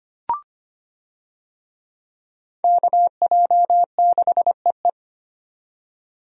Morse code, heard as a keyed tone, about 25 wpm, 700 Hz. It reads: KJ6EE
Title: USA Call Signs - 5 Characters - 25wpm